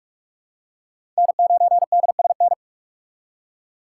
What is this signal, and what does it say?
Morse code, heard as a keyed tone, about 42 wpm, 700 Hz. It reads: N9DSN
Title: USA Call Signs - 5 Characters - 4x Character Spacing - 45wpm